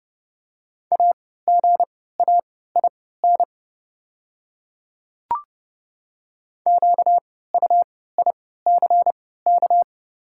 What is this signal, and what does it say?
Morse code, heard as a keyed tone, about 30 wpm, spaced as for 17 wpm, 700 Hz. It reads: AGAIN QUICK
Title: 3x Character Spacing - Top 500 Words - 30wpm